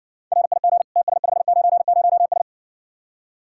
Morse code, heard as a keyed tone, about 68 wpm, 700 Hz. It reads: RIG TS590S